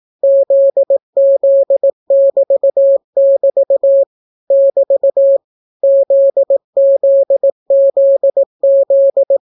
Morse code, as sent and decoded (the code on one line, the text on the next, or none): --.. --.. -...- -...- / -...- / --.. --.. --.. --..
ZZ<BT><BT> <BT> ZZZZ